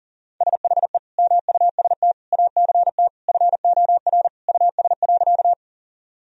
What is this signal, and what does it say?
Morse code, heard as a keyed tone, about 37 wpm, 700 Hz. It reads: SHE MUST ACT FOR US.
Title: Sentences from Top 1000 Words - Review Part 1 - 40wpm